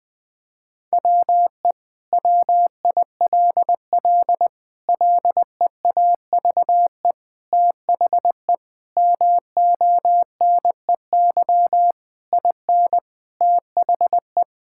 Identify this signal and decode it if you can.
Morse code, heard as a keyed tone, about 21 wpm, 700 Hz. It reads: WE WILL LEAVE THE MONEY IN THE
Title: Sentences from Top 1000 Words - Review Part 6 - 20wpm